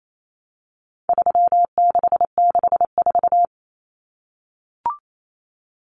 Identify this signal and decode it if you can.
Morse code, heard as a keyed tone, about 28 wpm, 700 Hz. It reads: 3664
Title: Common Cut Numbers - 4-Digit 28wpm